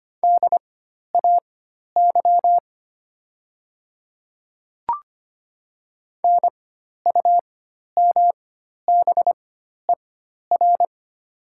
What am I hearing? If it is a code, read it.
Morse code, heard as a keyed tone, about 25 wpm, spaced as for 12 wpm, 700 Hz. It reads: DAY NUMBER
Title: 4x Character Spacing - Top 100 Words - 25wpm